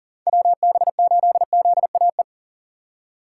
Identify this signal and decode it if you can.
Morse code, heard as a keyed tone, about 40 wpm, 700 Hz. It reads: WB8ZAE
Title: USA Call Signs - 6 Characters - 3x Character Spacing - 40wpm